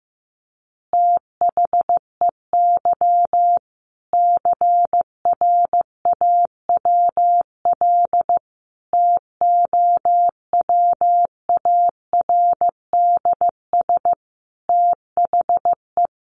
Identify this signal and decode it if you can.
Morse code, heard as a keyed tone, about 15 wpm, 700 Hz. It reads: THEY CRAWL TOWARDS THE
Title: Additional Easy Phrases 15wpm